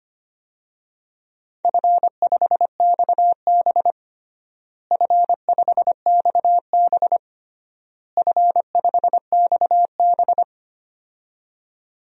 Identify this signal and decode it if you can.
Morse code, heard as a keyed tone, about 25 wpm, 700 Hz. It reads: F5XB F5XB F5XB
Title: DXCC-100 Call Signs - 4 Characters - Repeated 3x - 25wpm